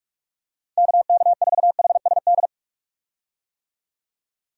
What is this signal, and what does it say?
Morse code, heard as a keyed tone, about 47 wpm, 700 Hz. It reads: KK4HSD
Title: USA Call Signs - 6 Characters - 45wpm